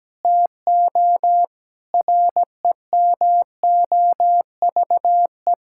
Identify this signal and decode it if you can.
Morse code, heard as a keyed tone, about 17 wpm, 700 Hz. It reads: TO REMOVE
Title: Sentences from Top 1000 Words - Review Part 5 - 17wpm